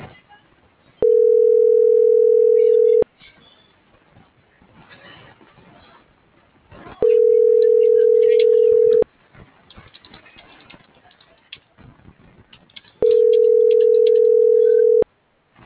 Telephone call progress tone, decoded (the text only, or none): ringback tone